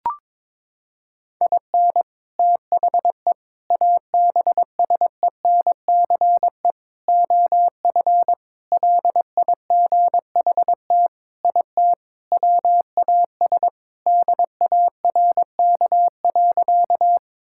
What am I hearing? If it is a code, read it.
Morse code, heard as a keyed tone, about 22 wpm, 700 Hz. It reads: IN THE ABSENCE OF LIGHT IT WAS DARK.
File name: practice / 22wpm / Sentences with 4-Word Phrases - 22wpm